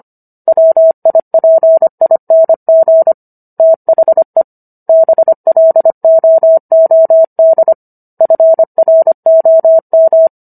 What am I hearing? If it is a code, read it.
Morse code, heard as a keyed tone, about 25 wpm, 650 Hz. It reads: WIPING THE BLOOD FROM